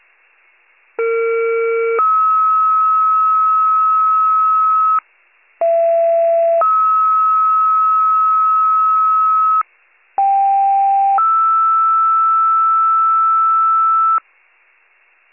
Two-tone page, heard as a one-second tone, then a three-second tone, about 460 then 1290 Hz, about 670 then 1290 Hz, about 790 then 1320 Hz. It is heard from 1.0 to 14.2 s.